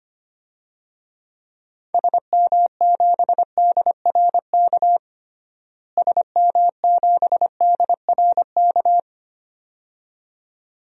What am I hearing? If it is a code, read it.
Morse code, heard as a keyed tone, about 25 wpm, 700 Hz. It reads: SM7DRK SM7DRK
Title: DXCC-100 Call Signs - 6 Characters - Repeated 2x - 25wpm